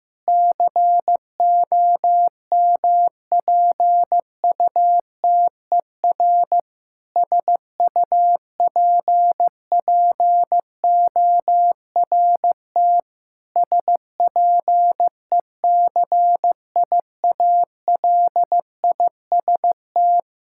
Morse code, heard as a keyed tone, about 15 wpm, 700 Hz. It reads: COMPUTER SUPPORT SPECIALIST
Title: Practice QSO - Exchange 3 - More Info - 15wpm